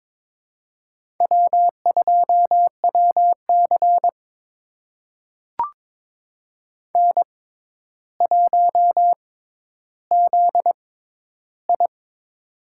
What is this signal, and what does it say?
Morse code, heard as a keyed tone, about 22 wpm, spaced as for 11 wpm, 700 Hz. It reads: W2WC N1ZI
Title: USA Call Signs - 6x Character Spacing - 22wpm